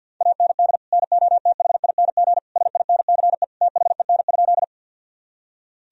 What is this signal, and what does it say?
Morse code, heard as a keyed tone, about 53 wpm, 700 Hz. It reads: AND NOTHING SINCE THEN?
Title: Sentences from Top 1000 Words - Review Part 1 - 50wpm